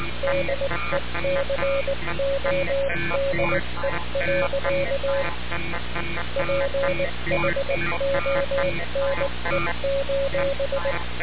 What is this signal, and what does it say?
Morse code, heard as a keyed tone, about 19 wpm, 550 Hz, with noise. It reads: DECODE. PHONE:
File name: radiocap / REC z